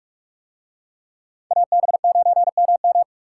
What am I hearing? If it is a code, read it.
Morse code, heard as a keyed tone, about 44 wpm, 700 Hz. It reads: AB9MM